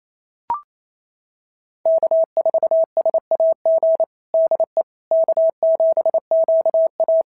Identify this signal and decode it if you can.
Morse code, heard as a keyed tone, about 29 wpm, 690 Hz. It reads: K4SAG DE K7QA
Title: SOTA QSO Practice 28wpm